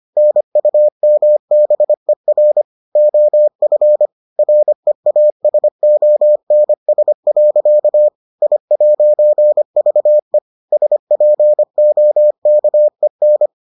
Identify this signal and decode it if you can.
Morse code, heard as a keyed tone, about 25 wpm, 600 Hz. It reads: NUMBER OF REASONS. I'VE SPOKEN